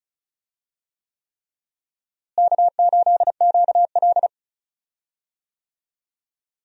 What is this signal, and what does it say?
Morse code, heard as a keyed tone, about 34 wpm, 700 Hz. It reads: K8QL
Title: USA Call Signs - 4 Characters - Repeated 3x - 35wpm